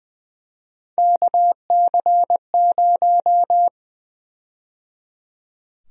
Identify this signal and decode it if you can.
Morse code, heard as a keyed tone, about 20 wpm, 700 Hz. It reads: KC0
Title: USA Call Sign Prefixes - 20wpm